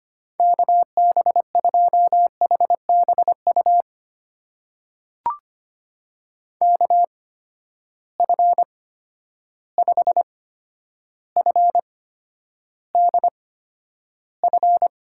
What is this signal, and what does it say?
Morse code, heard as a keyed tone, about 25 wpm, spaced as for 11 wpm, 700 Hz. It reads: KB2HBU KF5FDF
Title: USA Call Signs - 6 Characters - 8x Character Spacing - 25wpm